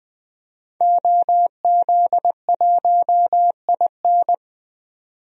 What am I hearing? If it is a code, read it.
Morse code, heard as a keyed tone, about 20 wpm, 700 Hz. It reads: OZ1IN
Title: DXCC-100 Call Signs - 5 Characters - 20wpm